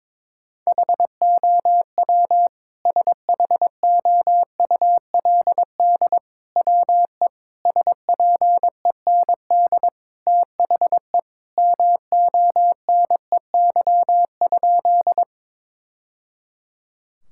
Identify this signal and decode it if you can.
Morse code, heard as a keyed tone, about 22 wpm, 700 Hz. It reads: HOW SHOULD WE SPEND THE MONEY?
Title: Sentences from Top 1000 Words - Review Part 5 - 22wpm